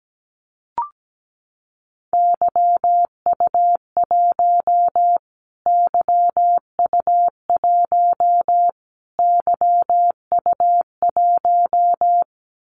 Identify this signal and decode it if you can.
Morse code, heard as a keyed tone, about 17 wpm, 700 Hz. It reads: YU1 YU1 YU1
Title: Callsign Prefixes for Top 25 DXCC - Repeated 3x - 17wpm